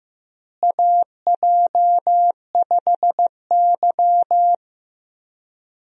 Morse code, heard as a keyed tone, about 15 wpm, 700 Hz. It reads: AJ5Y
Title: USA Call Signs - 4 Characters - Repeated 3x - 15wpm